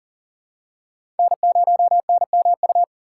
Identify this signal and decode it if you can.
Morse code, heard as a keyed tone, about 40 wpm, 700 Hz. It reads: N0NMU
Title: USA Call Signs - 5 Characters - 3x Character Spacing - 40wpm